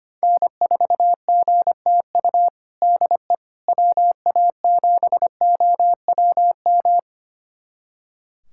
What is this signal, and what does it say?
Morse code, heard as a keyed tone, about 25 wpm, 700 Hz. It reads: N4GTU DE WA7OWM
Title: USA Callsign DE Callsign 25wpm